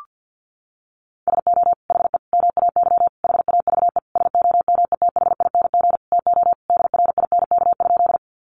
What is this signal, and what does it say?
Morse code, heard as a keyed tone, about 49 wpm, 700 Hz. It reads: SO HE MAY HAVE SOMETHING TO DRINK?
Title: Sentences from Top 1500 Words 50wpm